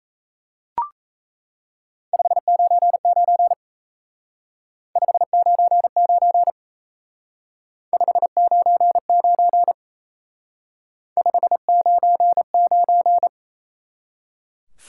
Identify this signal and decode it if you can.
Morse code, heard as a keyed tone, about 33 wpm, 700 Hz. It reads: 599 599 599 599
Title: Speed-Racing Top 90 QSO Elements 28wpm